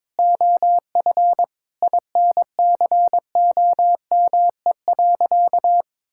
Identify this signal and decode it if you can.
Morse code, heard as a keyed tone, about 22 wpm, 700 Hz. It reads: OF INCOME.
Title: Sentences from Top 1000 Words - Review Part 5 - 22wpm